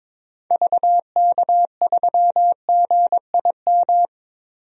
Morse code, heard as a keyed tone, about 21 wpm, 700 Hz. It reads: VK3GIM
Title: DXCC-100 Call Signs - 6 Characters - 22wpm